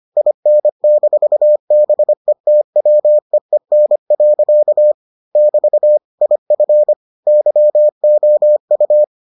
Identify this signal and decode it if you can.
Morse code, heard as a keyed tone, about 25 wpm, 600 Hz. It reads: IN-BETWEEN. <BT> IF YOU